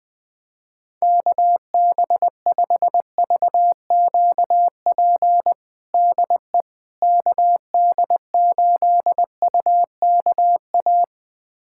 Morse code, heard as a keyed tone, about 20 wpm, 700 Hz. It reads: KB5VQP DE KD8UKA
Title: USA Callsign DE Callsign - 6 Characters - 20wpm